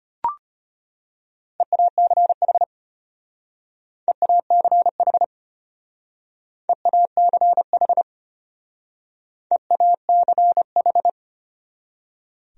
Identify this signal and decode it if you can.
Morse code, heard as a keyed tone, about 32 wpm, 700 Hz. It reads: EACH EACH EACH EACH